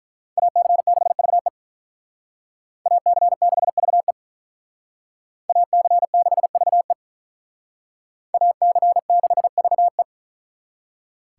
Morse code, heard as a keyed tone, about 44 wpm, 700 Hz. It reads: AC6VE AC6VE AC6VE AC6VE